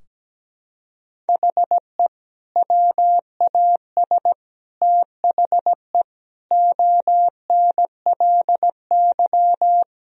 Morse code, heard as a keyed tone, about 17 wpm, 700 Hz. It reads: HE WAS THE ONLY
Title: Sentences from Top 1000 Words - Review Part 3 - 17wpm